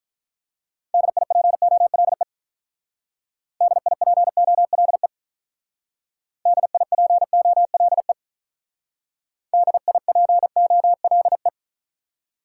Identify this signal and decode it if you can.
Morse code, heard as a keyed tone, about 44 wpm, 700 Hz. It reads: DIPOLE DIPOLE DIPOLE DIPOLE
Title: Speed-Racing Top 90 QSO Elements 35wpm